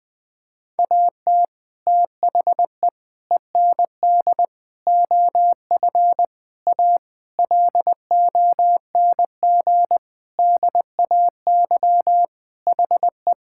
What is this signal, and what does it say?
Morse code, heard as a keyed tone, about 20 wpm, 700 Hz. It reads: AT THE END OF A LONG DAY HE
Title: Sentences with 5-Word Phrases - 20wpm